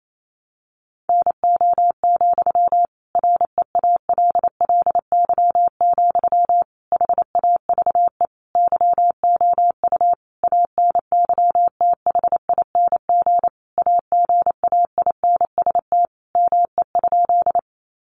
Morse code, heard as a keyed tone, about 28 wpm, 700 Hz. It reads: NO, REALLY, HAVE YOU ANYTHING AGAINST ME?
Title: Sentences from Top 1000 Words 28wpm